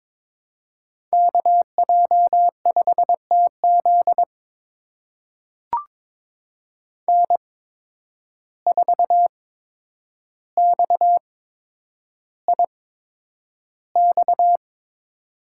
Morse code, heard as a keyed tone, about 21 wpm, spaced as for 9 wpm, 700 Hz. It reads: KJ5TZ N4XIX
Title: USA Call Signs - 5 Characters - 8x Character Spacing - 22wpm